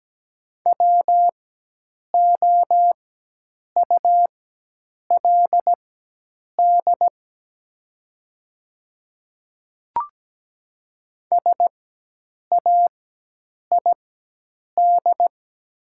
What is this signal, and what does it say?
Morse code, heard as a keyed tone, about 17 wpm, spaced as for 8 wpm, 700 Hz. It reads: WOULD SAID
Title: 4x Character Spacing - Top 100 Words - 17wpm